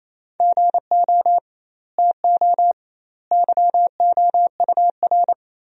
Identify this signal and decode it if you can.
Morse code, heard as a keyed tone, about 28 wpm, 700 Hz. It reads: GO TO YOUR
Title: 2x Word-Spacing - Sentences from Top 300 Words - 28wpm